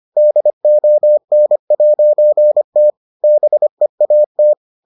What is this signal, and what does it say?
Morse code, heard as a keyed tone, about 25 wpm, 600 Hz. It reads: DON'T BEAT